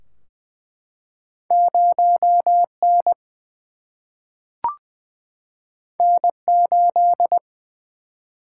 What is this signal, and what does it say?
Morse code, heard as a keyed tone, about 20 wpm, 700 Hz. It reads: 0 N N 8
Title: Single-Letter Single-Number 20wpm